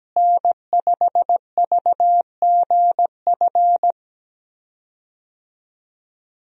Morse code, heard as a keyed tone, about 17 wpm, 700 Hz. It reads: N5VGF
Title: USA Call Signs - 5 Characters - Repeated 3x - 17wpm